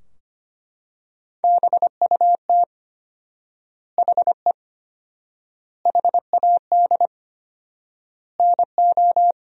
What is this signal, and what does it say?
Morse code, heard as a keyed tone, about 25 wpm, 700 Hz. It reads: BUT HE HAD NO
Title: 4x Word-Spacing - Sentences from Top 100 Words - 25wpm